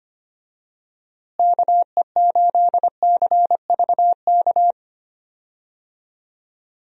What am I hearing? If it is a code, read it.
Morse code, heard as a keyed tone, about 25 wpm, 700 Hz. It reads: KE8CVK
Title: USA Call Signs - 6 Characters - Repeated 2x - 25wpm